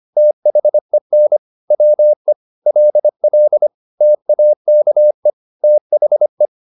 Morse code, heard as a keyed tone, about 25 wpm, 600 Hz. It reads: THEN WE LL TAKE THE
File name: the old man and the sea part 1 25wpm